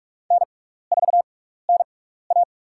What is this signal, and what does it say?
Morse code, heard as a keyed tone, about 44 wpm, spaced as for 15 wpm, 700 Hz. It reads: N4NA
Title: USA Call Signs - 4 Characters - 6x Character Spacing - 45wpm